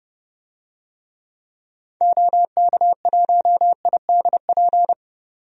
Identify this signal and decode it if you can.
Morse code, heard as a keyed tone, about 30 wpm, 700 Hz. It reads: OK1IDP
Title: DXCC-100 Call Signs - 6 Characters - 30wpm